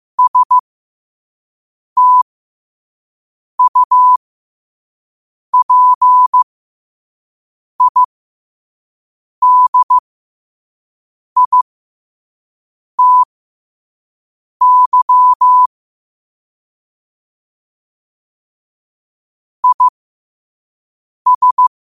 Morse code, heard as a keyed tone, about 15 wpm, spaced as for 5 wpm, 1000 Hz. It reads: STUPIDITY IS
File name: qotdfiles / 5wpm